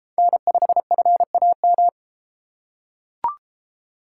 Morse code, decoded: N5FAM

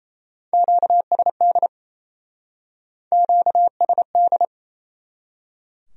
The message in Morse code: QSD QSD

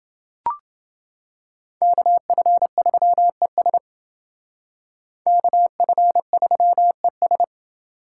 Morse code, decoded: KF3ES KF3ES

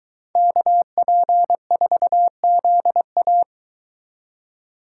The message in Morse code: KP4ZA

